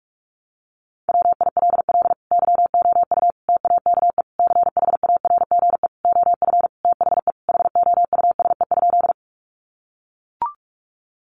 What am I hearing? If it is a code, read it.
Morse code, heard as a keyed tone, about 44 wpm, 700 Hz. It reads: WILL YOU TAKE CHARGE OF THE HOUSE?
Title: Sentences from Top 900 Words 45wpm